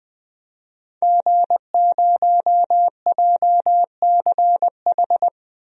Morse code, heard as a keyed tone, about 20 wpm, 700 Hz. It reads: G0JCH
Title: DXCC-100 Call Signs - 5 Characters - Repeated 2x - 20wpm